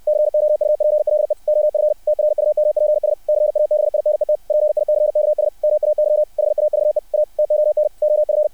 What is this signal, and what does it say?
Morse code, heard as a keyed tone, about 60 wpm, 600 Hz. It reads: COUPLE OF TRUMPS CABINET PICKS MAY HAVE A TON OF
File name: archived / CRSnet